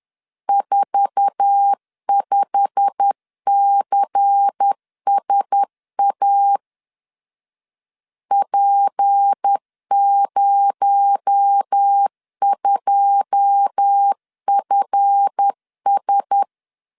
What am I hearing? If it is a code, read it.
Morse code, heard as a keyed tone, about 10 wpm, 800 Hz. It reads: 45CSA P02FS
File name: LEZIONI / 9 2 10wpm 2736